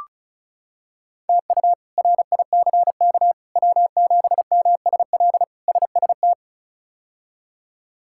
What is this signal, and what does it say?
Morse code, heard as a keyed tone, about 34 wpm, 700 Hz. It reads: TU RICK W7MSL SST
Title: K1USN SST QSO Practice - 35wpm